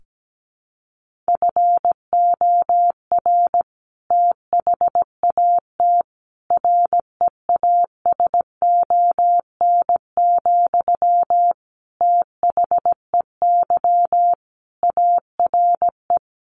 Morse code, heard as a keyed tone, about 17 wpm, 700 Hz. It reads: FOR THAT REASON, THEY ARE
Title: Sentences from Top 1500 Words 17wpm